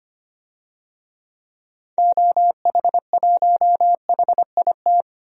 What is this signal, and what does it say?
Morse code, heard as a keyed tone, about 25 wpm, 700 Hz. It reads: OH1HIT
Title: DXCC-100 Call Signs - 6 Characters - 25wpm